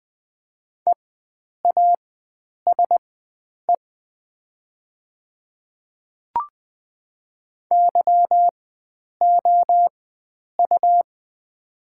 Morse code, heard as a keyed tone, about 20 wpm, spaced as for 9 wpm, 700 Hz. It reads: EASE YOU
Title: 4x Character Spacing - Top 300 Words - 20wpm